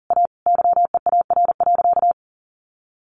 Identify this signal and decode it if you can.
Morse code, heard as a keyed tone, about 40 wpm, 700 Hz. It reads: A YEAR.
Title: Sentences from Top 700 Words 40wpm